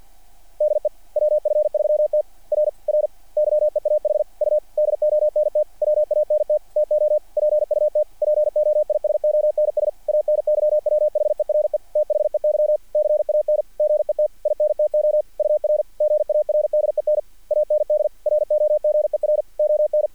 Morse code, heard as a keyed tone, about 48 wpm, 600 Hz. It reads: DE WF3T R R YEAH U DONT WANT TO PUT POISONS ANYWHERE THEY CAN GET INTO UR GARDEN AND ROGER ON
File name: archived / CRSnet